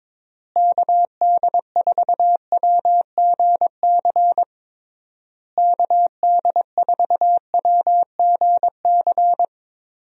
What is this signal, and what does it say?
Morse code, heard as a keyed tone, about 22 wpm, 700 Hz. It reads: KD4WGC KD4WGC
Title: USA Call Signs - 6 Characters - Repeated 3x - 22wpm